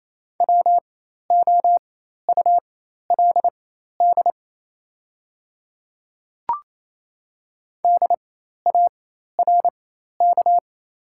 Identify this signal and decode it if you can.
Morse code, heard as a keyed tone, about 27 wpm, spaced as for 13 wpm, 700 Hz. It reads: WOULD DARK
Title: 4x Character Spacing - Top 500 Words - 28wpm